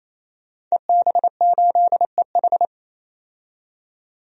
Morse code, heard as a keyed tone, about 28 wpm, 700 Hz. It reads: EB8EH
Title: DXCC-100 Call Signs - 5 Characters - 28wpm